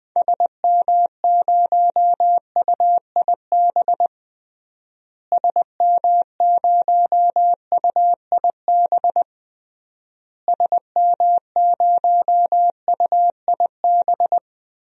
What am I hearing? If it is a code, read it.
Morse code, heard as a keyed tone, about 20 wpm, 700 Hz. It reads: SM0UIB SM0UIB SM0UIB
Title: DXCC-100 Call Signs - 6 Characters - Repeated 3x - 20wpm